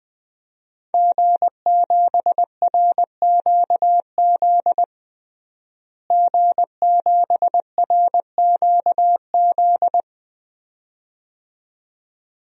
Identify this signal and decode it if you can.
Morse code, heard as a keyed tone, about 20 wpm, 700 Hz. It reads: G7RQZ G7RQZ
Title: DXCC-100 Call Signs - 5 Characters - Repeated 2x - 20wpm